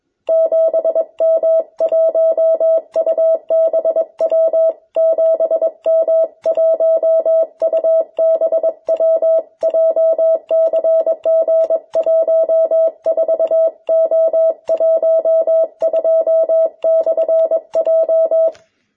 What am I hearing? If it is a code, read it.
Morse code, heard as a keyed tone, about 21 wpm, 650 Hz. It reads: 7M1UBW 7M1UBW JCG14O12/J
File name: jh8cbh / 2021cw